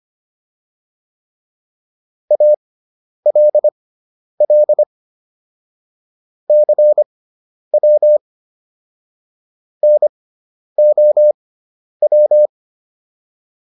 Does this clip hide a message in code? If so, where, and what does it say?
Yes: Morse code from 2.3 to 12.5 s, heard as a keyed tone, about 25 wpm, spaced as for 10 wpm, 600 Hz. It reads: ALL CW NOW